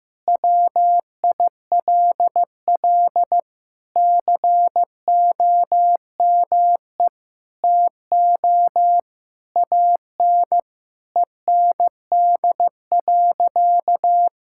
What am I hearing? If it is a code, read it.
Morse code, heard as a keyed tone, about 15 wpm, 700 Hz. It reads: WILL COME TO AN END.
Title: Sentences from Top 1000 Words - Review Part 6 - 15wpm